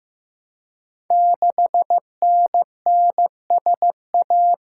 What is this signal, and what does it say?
Morse code, heard as a keyed tone, about 15 wpm, 700 Hz. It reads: 6NNSA
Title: Code Groups - 5 Letter:Numbers - 15wpm